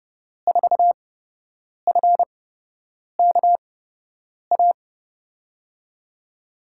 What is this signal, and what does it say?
Morse code, heard as a keyed tone, about 30 wpm, spaced as for 8 wpm, 700 Hz. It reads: 4FKA
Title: USA Call Signs - 6 Characters - 8x Character Spacing - 30wpm